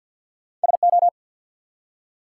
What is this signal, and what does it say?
Morse code, heard as a keyed tone, about 50 wpm, 700 Hz. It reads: SO
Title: Two Letter Words - 50wpm